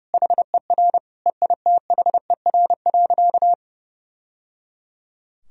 Morse code, heard as a keyed tone, about 30 wpm, 700 Hz. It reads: HER EITHER.
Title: Sentences from Top 1000 Words - Review Part 4 - 30wpm